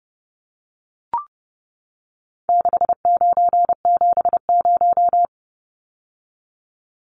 Morse code, heard as a keyed tone, about 30 wpm, 700 Hz. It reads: 6970
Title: Numbers - 4-Digit 30wpm